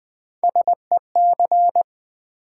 Morse code, heard as a keyed tone, about 20 wpm, 700 Hz. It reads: SEC
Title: Build A Word - 3 Letters At A Time - 9-Letter Words - 20wpm